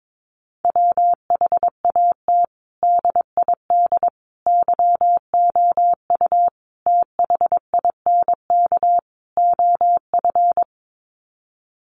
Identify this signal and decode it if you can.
Morse code, heard as a keyed tone, about 22 wpm, 700 Hz. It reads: WHAT DID YOU THINK OF
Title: Sets of 5 Words Encore2 22wpm